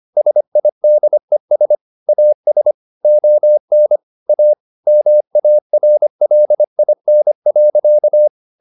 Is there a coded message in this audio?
Yes: Morse code from 0.2 to 8.3 s, heard as a keyed tone, about 25 wpm, 600 Hz. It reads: SIDES AS ON A MARLIN.